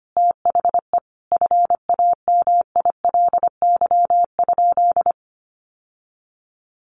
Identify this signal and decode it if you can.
Morse code, heard as a keyed tone, about 25 wpm, 700 Hz. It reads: THE FAMILY?
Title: Sentences from Top 1100 Words 25wpm